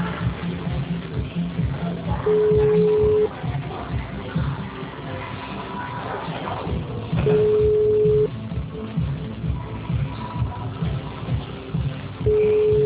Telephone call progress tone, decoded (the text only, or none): ringback tone